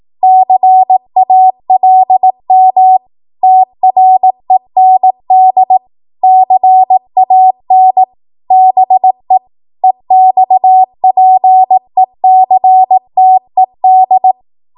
Morse code, heard as a keyed tone, about 18 wpm, 750 Hz. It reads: CALM TREND CAN BE EXPECTED